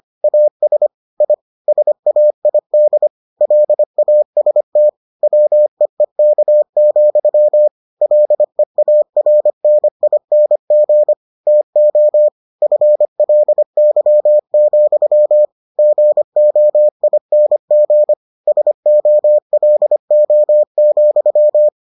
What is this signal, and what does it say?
Morse code, heard as a keyed tone, about 25 wpm, 600 Hz. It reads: AS I SAID LAST WEEK, LEARNING TO FLY, GOING SOLO,